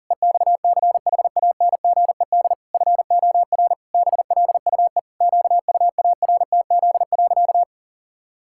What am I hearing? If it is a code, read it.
Morse code, heard as a keyed tone, about 37 wpm, 700 Hz. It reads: EXCHANGED FOR BLUE QUARTZ.
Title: Pangram Challenge v2 - 40wpm